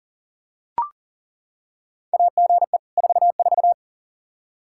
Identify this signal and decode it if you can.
Morse code, heard as a keyed tone, about 40 wpm, 700 Hz. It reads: AGE 44
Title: Practice QSO - Exchange 3 - More Info - 40wpm